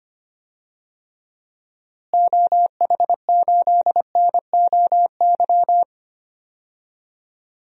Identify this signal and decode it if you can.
Morse code, heard as a keyed tone, about 25 wpm, 700 Hz. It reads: OH8NOY
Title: DXCC-100 Call Signs - 6 Characters - Repeated 3x - 25wpm